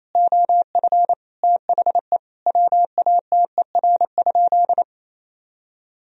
Morse code, heard as a keyed tone, about 28 wpm, 700 Hz. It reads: OF THE WATER?
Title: Sentences from Top 1000 Words - Review Part 3 - 28wpm